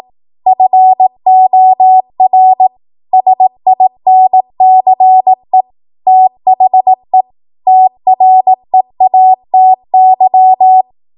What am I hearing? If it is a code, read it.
Morse code, heard as a keyed tone, about 18 wpm, 750 Hz. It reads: FOR SINCE THE TREATY
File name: Morse / 18WPM